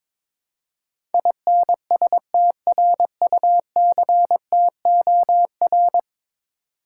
Morse code, heard as a keyed tone, about 22 wpm, 700 Hz. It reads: INSTRUCTOR
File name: practice / 22wpm / 10 Letter Words - First 3 Letters Given - 22wpm